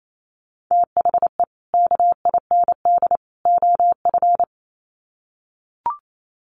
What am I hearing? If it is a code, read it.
Morse code, heard as a keyed tone, about 28 wpm, 700 Hz. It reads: THE KIND OF
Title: Sets of 3 Words 28wpm